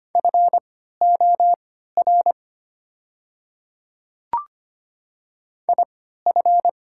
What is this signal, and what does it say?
Morse code, heard as a keyed tone, about 25 wpm, spaced as for 14 wpm, 700 Hz. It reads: FOR IF
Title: 3x Character Spacing - Top 100 Words - 25wpm